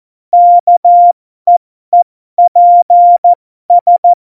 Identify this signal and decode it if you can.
Morse code, heard as a keyed tone, about 14 wpm, spaced as for 12 wpm, 700 Hz. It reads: KEEPS
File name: Morsekurs / Witze